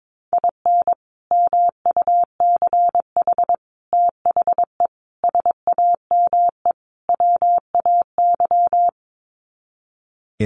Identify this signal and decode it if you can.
Morse code, heard as a keyed tone, about 22 wpm, 700 Hz. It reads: IN MUCH THE SAME WAY
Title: Sets of 5 Words Encore 22wpm